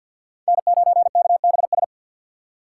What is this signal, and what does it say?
Morse code, heard as a keyed tone, about 47 wpm, 700 Hz. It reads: N9KBS
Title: USA Call Signs - 5 Characters - 4x Character Spacing - 50wpm